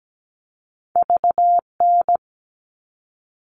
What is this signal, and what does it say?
Morse code, heard as a keyed tone, about 17 wpm, 700 Hz. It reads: VN
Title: All Cut Numbers - 2-Digit 17wpm